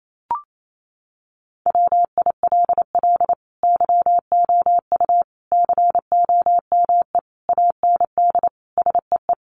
Morse code, heard as a keyed tone, about 28 wpm, 700 Hz. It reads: WILL YOU COME AND SEE
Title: Sentences from Top 100 Words 28wpm